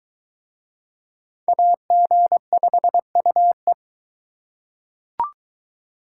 Morse code, heard as a keyed tone, about 23 wpm, 700 Hz. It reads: AG5UE